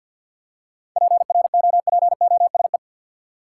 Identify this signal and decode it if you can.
Morse code, heard as a keyed tone, about 53 wpm, 700 Hz. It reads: PROPOSE
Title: Seven Letter Words - 50wpm